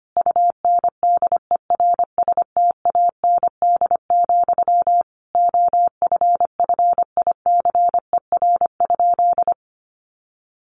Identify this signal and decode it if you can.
Morse code, heard as a keyed tone, about 25 wpm, 700 Hz. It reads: UNDERSTAND, OFFICER?
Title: Sentences from Top 1200 Words 25wpm